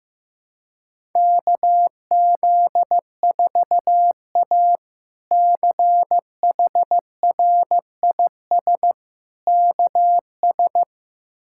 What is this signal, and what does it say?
Morse code, heard as a keyed tone, about 15 wpm, 700 Hz. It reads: KZ4A CHRIS KS
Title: K1USN SST QSO Practice - 15wpm